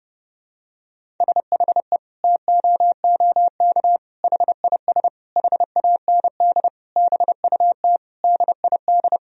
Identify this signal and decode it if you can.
Morse code, heard as a keyed tone, about 30 wpm, 700 Hz. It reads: SHE TOOK HIS HAND BUT DID
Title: Sentences from Top 1000 Words - Review Part 1 - 30wpm